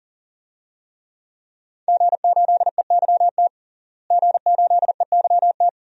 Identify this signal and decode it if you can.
Morse code, heard as a keyed tone, about 40 wpm, 700 Hz. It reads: G8EYT G8EYT
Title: DXCC-100 Call Signs - 5 Characters - Repeated 3x - 40wpm